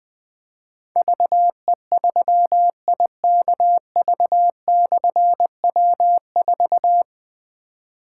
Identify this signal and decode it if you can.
Morse code, heard as a keyed tone, about 20 wpm, 700 Hz. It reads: VE3IKV/W4
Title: Difficult Call Signs 20wpm